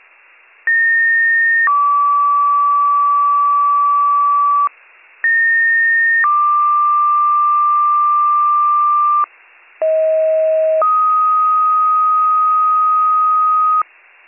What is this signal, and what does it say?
Two-tone page, heard as a one-second tone, then a three-second tone, about 1850 then 1150 Hz, about 1850 then 1180 Hz, about 640 then 1250 Hz, with noise. It is heard from 0.7 to 13.8 s.